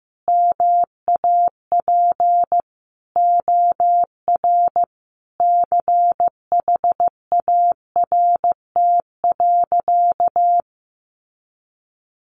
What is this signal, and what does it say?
Morse code, heard as a keyed tone, about 15 wpm, 700 Hz. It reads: MAP OR CHART.
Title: Sentences from Top 1300 Words 15wpm